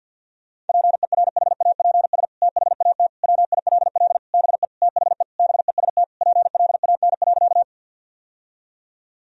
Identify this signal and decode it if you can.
Morse code, heard as a keyed tone, about 51 wpm, 700 Hz. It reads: PERHAPS THAT WILL BE THE BEST PLAN.
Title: Sentences from Top 1000 Words - Review Part 2 - 50wpm